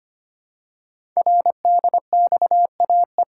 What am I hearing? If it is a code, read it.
Morse code, heard as a keyed tone, about 25 wpm, 700 Hz. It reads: RDXAE
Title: Code Groups - 5 Letters - 25wpm